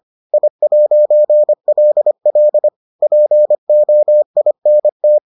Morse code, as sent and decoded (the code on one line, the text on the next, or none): .. .----. .-.. .-.. / .--. --- .. -. -
I'LL POINT